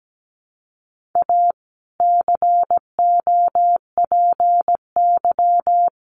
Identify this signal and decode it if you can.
Morse code, heard as a keyed tone, about 17 wpm, 700 Hz. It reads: A COPY
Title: Sentences from Top 1400 Words 17wpm